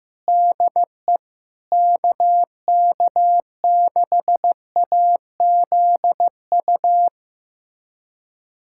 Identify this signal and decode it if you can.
Morse code, heard as a keyed tone, about 15 wpm, 700 Hz. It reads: DE KK6AZU
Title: USA Callsign DE Callsign 15wpm